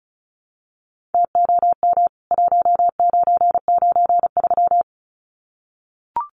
Morse code, heard as a keyed tone, about 34 wpm, 700 Hz. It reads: TOM 1993
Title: CWT Exchange 35wpm